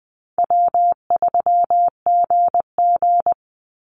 Morse code, heard as a keyed tone, about 20 wpm, 700 Hz. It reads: W3GG
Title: Easy Call Signs Repeated 2x - 20wpm